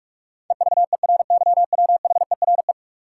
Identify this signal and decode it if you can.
Morse code, heard as a keyed tone, about 45 wpm, 700 Hz. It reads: EVERYWHERE
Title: Sentences from Top 1600 Words - New Word Drill Alt - 45wpm